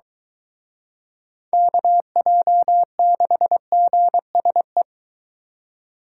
Morse code, heard as a keyed tone, about 22 wpm, 700 Hz. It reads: KJ6GSE